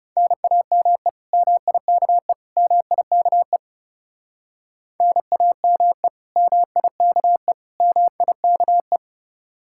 Morse code, heard as a keyed tone, about 37 wpm, 700 Hz. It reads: NAME MIKE MIKE NAME MIKE MIKE
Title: Speed-Racing Common QSO Phrases Encore - 30wpm